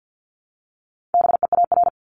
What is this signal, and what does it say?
Morse code, heard as a keyed tone, about 50 wpm, 700 Hz. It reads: BEAR
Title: Multi-letter Phonemes 50wpm